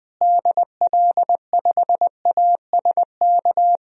Morse code, heard as a keyed tone, about 21 wpm, 700 Hz. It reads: DL5ASK
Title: DXCC-100 Call Signs - 6 Characters - Repeated 3x - 20wpm